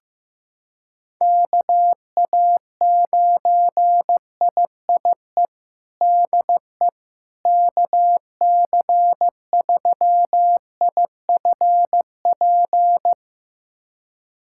Morse code, heard as a keyed tone, about 15 wpm, 700 Hz. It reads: KA9IIE DE KC3IFP
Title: USA Callsign DE Callsign 15wpm